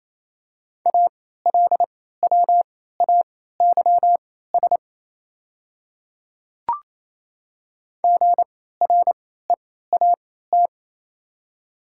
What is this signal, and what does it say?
Morse code, heard as a keyed tone, about 28 wpm, spaced as for 16 wpm, 700 Hz. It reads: ALWAYS GREAT
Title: 3x Character Spacing - Top 500 Words - 28wpm